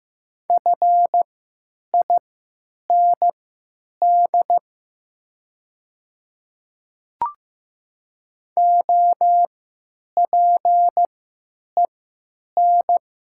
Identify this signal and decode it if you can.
Morse code, heard as a keyed tone, about 15 wpm, spaced as for 8 wpm, 700 Hz. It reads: FIND OPEN
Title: 3x Character Spacing - Top 300 Words - 15wpm